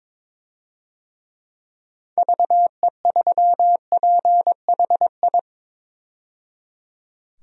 Morse code, heard as a keyed tone, about 22 wpm, 700 Hz. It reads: VE3PHI
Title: DXCC-100 Call Signs - 6 Characters - 22wpm